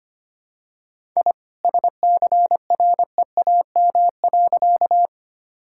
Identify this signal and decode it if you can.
Morse code, heard as a keyed tone, about 25 wpm, 700 Hz. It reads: I SCREAM.
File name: practice / 25wpm / All-Age Appropriate Jokes - 25wpm